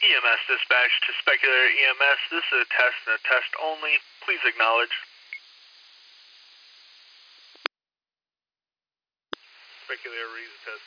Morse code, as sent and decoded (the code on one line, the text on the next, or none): none